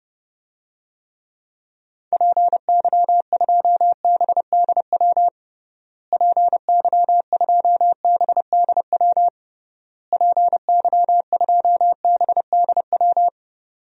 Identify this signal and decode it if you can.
Morse code, heard as a keyed tone, about 32 wpm, 700 Hz. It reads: PY2BDW PY2BDW PY2BDW
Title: DXCC-100 Call Signs - 6 Characters - Repeated 3x - 30wpm